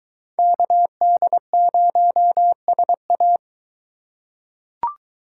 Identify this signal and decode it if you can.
Morse code, heard as a keyed tone, about 21 wpm, 700 Hz. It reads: KD0SA